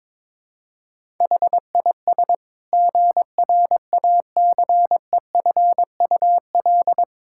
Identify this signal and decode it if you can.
Morse code, heard as a keyed tone, about 22 wpm, 700 Hz. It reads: HIS GRACEFUL
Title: Pangram Challenge v2 - 22wpm